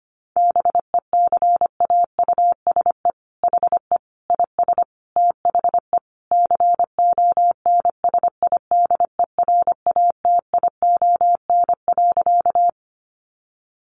Morse code, heard as a keyed tone, about 25 wpm, 700 Hz. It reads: BECAUSE HE IS THE CONSIDERATION.
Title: Sentences from Top 1500 Words 25wpm